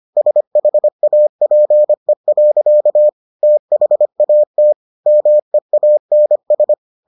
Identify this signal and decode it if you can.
Morse code, heard as a keyed tone, about 25 wpm, 600 Hz. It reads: SHAPE. THAT MEANS